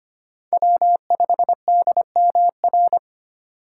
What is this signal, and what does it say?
Morse code, heard as a keyed tone, about 25 wpm, 700 Hz. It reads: W5DMR
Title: USA Call Signs - 2x Character Spacing - 25wpm